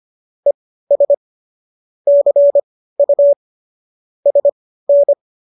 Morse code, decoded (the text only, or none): ES CU SN